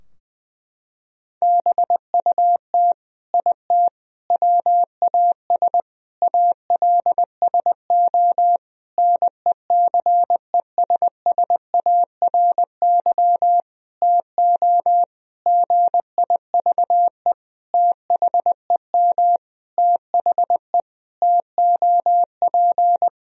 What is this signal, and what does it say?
Morse code, heard as a keyed tone, about 21 wpm, 700 Hz. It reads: BUT IT WAS ALSO NECESSARY TO GIVE THEM THE TOP
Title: Sentences from Top 1000 Words - Review Part 2 - 20wpm